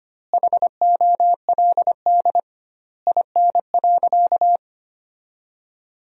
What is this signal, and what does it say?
Morse code, heard as a keyed tone, about 25 wpm, 700 Hz. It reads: HOLD IN.
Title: 2x Word-Spacing - Sentences from Top 400 Words - 25wpm